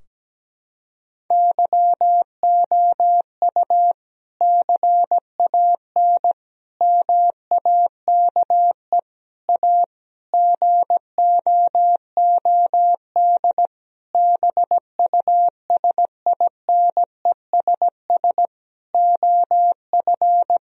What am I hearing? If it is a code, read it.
Morse code, heard as a keyed tone, about 17 wpm, 700 Hz. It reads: YOU CAN MAKE A GOOD BUSINESS OF
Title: Sentences from Top 1000 Words - Review Part 4 - 17wpm